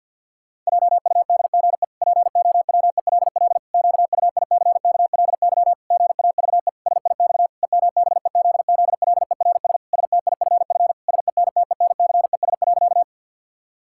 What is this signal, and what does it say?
Morse code, heard as a keyed tone, about 50 wpm, 700 Hz. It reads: JUDGE POWELL QUICKLY GAVE SIX EMBEZZLERS STIFF SENTENCES.
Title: Pangram Challenge v2 - 50wpm